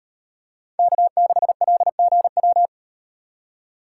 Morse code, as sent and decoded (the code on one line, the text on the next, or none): -.- -.... .-.. --. .--
K6LGW